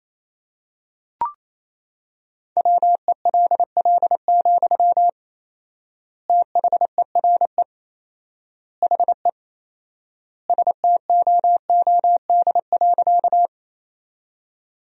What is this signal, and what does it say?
Morse code, heard as a keyed tone, about 28 wpm, 700 Hz. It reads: WELL, THERE HE STOOD.
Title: 4x Word-Spacing - Sentences from Top 500 Words - 28wpm